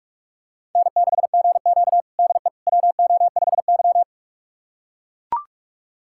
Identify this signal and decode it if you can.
Morse code, heard as a keyed tone, about 44 wpm, 700 Hz. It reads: N6GQ DE WO5Y
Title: USA Callsign DE Callsign - 4 Characters - 45wpm